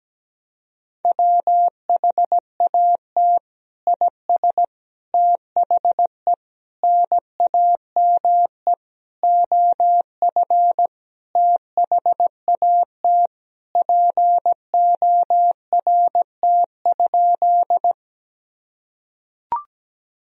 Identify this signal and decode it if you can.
Morse code, heard as a keyed tone, about 17 wpm, 700 Hz. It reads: WHAT IS THE NAME OF THAT PORT?
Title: Sentences from Top 1000 Words - Review Part 1 - 17wpm